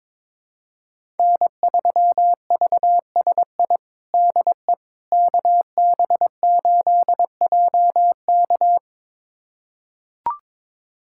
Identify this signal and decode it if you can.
Morse code, heard as a keyed tone, about 23 wpm, 700 Hz. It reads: N3VSI DE KB8JK
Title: USA Callsign DE Callsign - 5 Characters - 22wpm